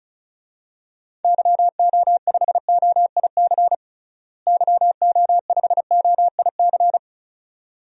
Morse code, heard as a keyed tone, about 34 wpm, 700 Hz. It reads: YO5OIC YO5OIC
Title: DXCC-100 Call Signs - 6 Characters - Repeated 2x - 35wpm